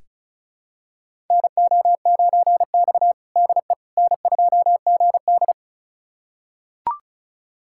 Morse code, heard as a keyed tone, about 36 wpm, 700 Hz. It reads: NO9X DE N2GD
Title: USA Callsign DE Callsign - 4 Characters - 35wpm